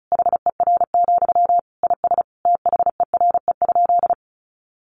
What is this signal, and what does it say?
Morse code, heard as a keyed tone, about 34 wpm, 700 Hz. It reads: HER, IS THERE?
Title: Sentences from Top 1500 Words 35wpm